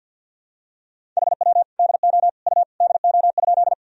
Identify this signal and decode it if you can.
Morse code, heard as a keyed tone, about 50 wpm, 700 Hz. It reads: HW DO U DO?
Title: Sentences from Top 100 Words and Abbreviations 50wpm